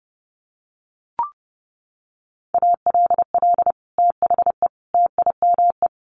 Morse code, heard as a keyed tone, about 30 wpm, 700 Hz. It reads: ALL THE TIME
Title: Sets of 3 Words 30wpm